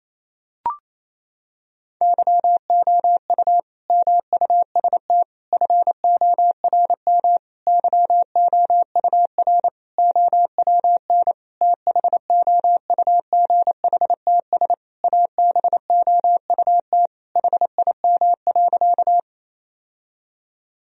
Morse code, heard as a keyed tone, about 28 wpm, 700 Hz. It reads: YOU MUST FORM YOUR OWN THOUGHTS ABOUT HIM.
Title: Sentences from Top 1000 Words - Review Part 1 - 28wpm